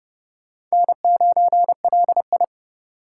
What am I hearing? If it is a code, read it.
Morse code, heard as a keyed tone, about 36 wpm, 700 Hz. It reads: N9LI
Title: Speed Racing - USA 1x2 Call Signs 30wpm